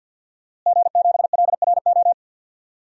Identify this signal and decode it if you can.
Morse code, heard as a keyed tone, about 52 wpm, 700 Hz. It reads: G7LRO
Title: DXCC-100 Call Signs - 5 Characters - Repeated 3x - 50wpm